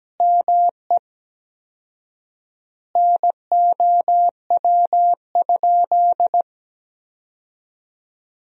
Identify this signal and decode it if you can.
Morse code, heard as a keyed tone, about 17 wpm, 700 Hz. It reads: ME NOW?
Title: 4x Word-Spacing - Sentences from Top 400 Words - 17wpm